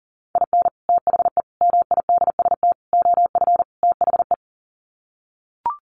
Morse code, heard as a keyed tone, about 40 wpm, 700 Hz. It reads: IN THE MIDST OF THE
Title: Sets of 5 Words 40wpm